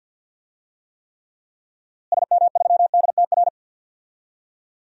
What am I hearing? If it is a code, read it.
Morse code, heard as a keyed tone, about 53 wpm, 700 Hz. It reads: SM3DTR
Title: DXCC-100 Call Signs - 6 Characters - Repeated 3x - 50wpm